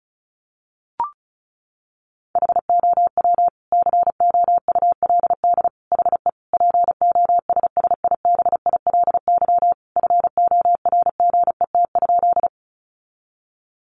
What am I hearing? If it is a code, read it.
Morse code, heard as a keyed tone, about 34 wpm, 700 Hz. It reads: HOW COULD HE POSSIBILY FORGET?
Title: Sentences from Top 1300 Words 35wpm